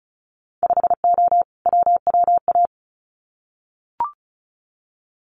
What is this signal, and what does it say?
Morse code, heard as a keyed tone, about 36 wpm, 700 Hz. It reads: 5O WWA
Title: Winter Field Day Exchanges Encore 35wpm